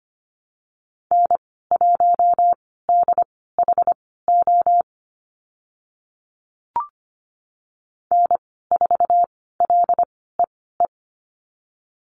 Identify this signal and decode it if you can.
Morse code, heard as a keyed tone, about 25 wpm, spaced as for 16 wpm, 700 Hz. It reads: N1DHO N4LEE
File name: Lesson 15g - Number 5 - Call Signs 25wpm